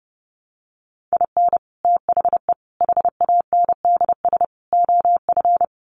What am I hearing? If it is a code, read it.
Morse code, heard as a keyed tone, about 30 wpm, 700 Hz. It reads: IN THE HANDS OF
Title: Sets of 4 Words 30wpm